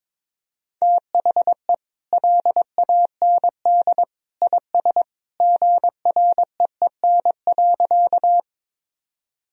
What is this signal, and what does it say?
Morse code, heard as a keyed tone, about 22 wpm, 700 Hz. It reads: THE LAND IS GREEN.
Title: Sentences from Top 1000 Words - Review Part 2 - 22wpm